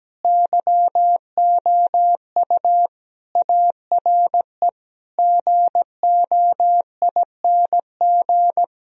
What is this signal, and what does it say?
Morse code, heard as a keyed tone, about 17 wpm, 700 Hz. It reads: YOU ARE GOING
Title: Sentences from Top 1000 Words - Review Part 2 - 17wpm